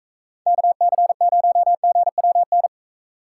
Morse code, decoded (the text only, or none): KC0GWN